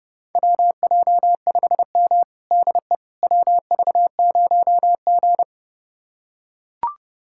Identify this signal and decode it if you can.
Morse code, heard as a keyed tone, about 30 wpm, 700 Hz. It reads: WJ5M DE WV0G
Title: USA Callsign DE Callsign - 4 Characters - 30wpm